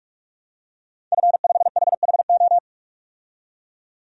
Morse code, heard as a keyed tone, about 44 wpm, 700 Hz. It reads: F5HHO
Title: DXCC-100 Call Signs - 5 Characters - 45wpm